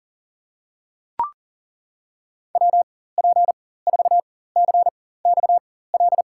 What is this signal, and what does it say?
Morse code, heard as a keyed tone, about 40 wpm, spaced as for 19 wpm, 700 Hz. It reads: WP4CXL
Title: USA Call Signs - 6 Characters - 4x Character Spacing - 40wpm